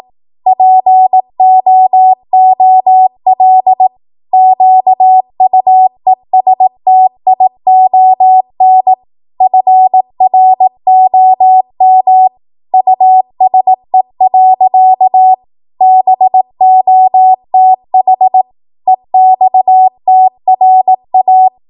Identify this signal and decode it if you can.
Morse code, heard as a keyed tone, about 18 wpm, 750 Hz. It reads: POOL QUESTION FROM USE. BOTH EXTRA